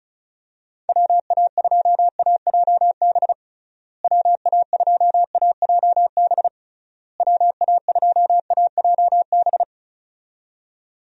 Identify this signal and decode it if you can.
Morse code, heard as a keyed tone, about 34 wpm, 700 Hz. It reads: WA2AJB WA2AJB WA2AJB
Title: USA Call Signs - 6 Characters - Repeated 3x - 35wpm